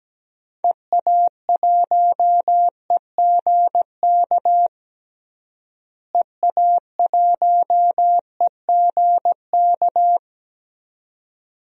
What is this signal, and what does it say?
Morse code, heard as a keyed tone, about 17 wpm, 700 Hz. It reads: EA1EGK EA1EGK
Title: DXCC-100 Call Signs - 6 Characters - Repeated 2x - 17wpm